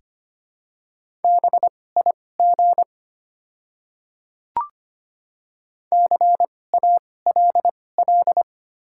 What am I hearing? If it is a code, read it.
Morse code, heard as a keyed tone, about 25 wpm, spaced as for 18 wpm, 700 Hz. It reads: BIG CALL
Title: 2x Character Spacing - Top 200 Words - 25wpm